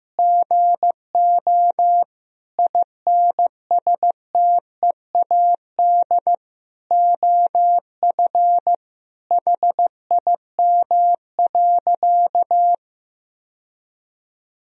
Morse code, heard as a keyed tone, about 15 wpm, 700 Hz. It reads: GO INSTEAD OF HIM.
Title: Sentences from Top 1000 Words - Review Part 4 - 15wpm